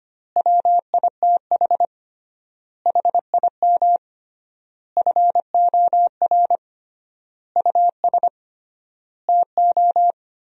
Morse code, heard as a keyed tone, about 25 wpm, 700 Hz. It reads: WITH HIM FOR US TO
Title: 3x Word-Spacing - Sentences from Top 500 Words - 25wpm